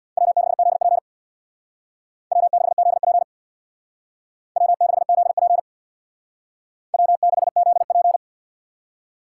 Morse code, heard as a keyed tone, about 64 wpm, 700 Hz. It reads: W6ZP W6ZP W6ZP W6ZP